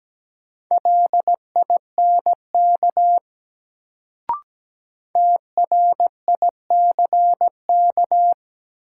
Morse code, heard as a keyed tone, about 17 wpm, 700 Hz. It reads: LINK TRICK
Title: Sentences from Top 1500 Words - New Word Drill Alt - 17wpm